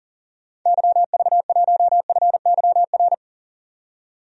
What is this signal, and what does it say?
Morse code, heard as a keyed tone, about 40 wpm, 700 Hz. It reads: YV1FYR
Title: DXCC-100 Call Signs - 6 Characters - 40wpm